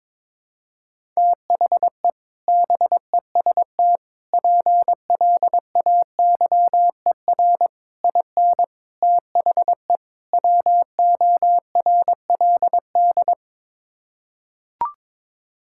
Morse code, heard as a keyed tone, about 22 wpm, 700 Hz. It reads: THE BEST PLAYER IN THE WORLD
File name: practice / 22wpm / Sets of 6 Words - 22wpm